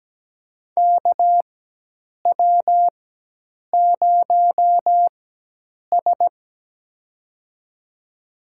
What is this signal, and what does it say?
Morse code, heard as a keyed tone, about 17 wpm, spaced as for 8 wpm, 700 Hz. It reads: KW0S
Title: USA Call Signs - 4x Character Spacing - 17wpm